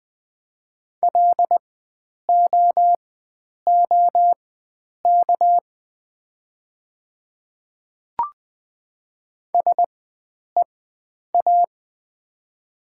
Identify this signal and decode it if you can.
Morse code, heard as a keyed tone, about 20 wpm, spaced as for 9 wpm, 700 Hz. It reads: LOOK SEA
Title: 4x Character Spacing - Top 500 Words - 20wpm